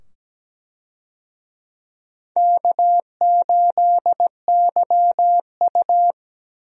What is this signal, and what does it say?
Morse code, heard as a keyed tone, about 17 wpm, 700 Hz. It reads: K8YU
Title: USA Call Signs - 4 Characters - 17wpm